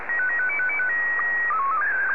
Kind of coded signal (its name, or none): SSTV (slow-scan television)